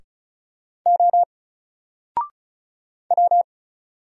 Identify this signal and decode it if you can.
Morse code, heard as a keyed tone, about 36 wpm, 700 Hz. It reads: O W
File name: Single-Letter-Number-35wpm